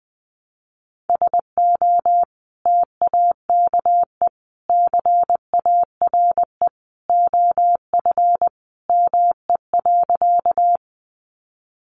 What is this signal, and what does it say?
Morse code, heard as a keyed tone, about 20 wpm, 700 Hz. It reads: SO TAKE CARE OF ME.
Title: Sentences from Top 300 Words 20wpm